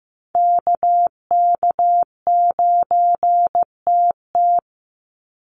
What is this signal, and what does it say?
Morse code, heard as a keyed tone, about 15 wpm, 700 Hz. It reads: KK9TT
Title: USA Call Signs 15wpm